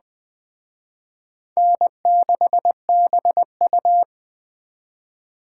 Morse code, heard as a keyed tone, about 21 wpm, 700 Hz. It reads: N6BU